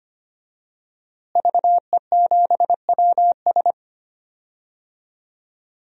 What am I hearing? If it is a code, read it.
Morse code, heard as a keyed tone, about 25 wpm, 700 Hz. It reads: VE7WS
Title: DXCC-100 Call Signs - 5 Characters - Repeated 2x - 25wpm